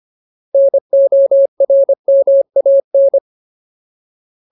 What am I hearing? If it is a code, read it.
Morse code, heard as a keyed tone, about 25 wpm, 550 Hz. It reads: NORMAN
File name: Ultimate Word List / 25WPM Ultimate